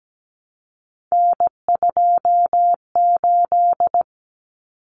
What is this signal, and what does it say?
Morse code, heard as a keyed tone, about 17 wpm, 690 Hz. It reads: N28
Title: Common Cut Numbers - 3-Digit 17wpm